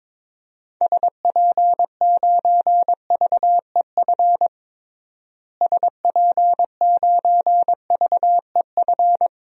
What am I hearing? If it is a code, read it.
Morse code, heard as a keyed tone, about 22 wpm, 700 Hz. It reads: SP9VEF SP9VEF
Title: DXCC-100 Call Signs - 6 Characters - Repeated 2x - 22wpm